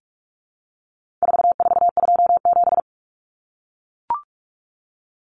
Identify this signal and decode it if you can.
Morse code, heard as a keyed tone, about 44 wpm, 700 Hz. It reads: 4427
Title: Numbers - 4-Digit 45wpm